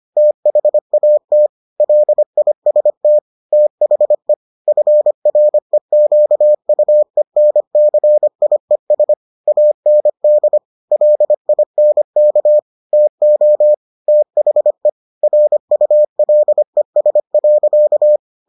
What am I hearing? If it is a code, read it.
Morse code, heard as a keyed tone, about 25 wpm, 600 Hz. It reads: THAT LIST THE FREQUENCIES AND LINK TO THE RULES.